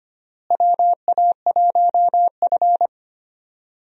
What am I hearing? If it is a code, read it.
Morse code, heard as a keyed tone, about 25 wpm, 700 Hz. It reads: WA1F
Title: USA Call Signs - 4 Characters - Repeated 2x - 25wpm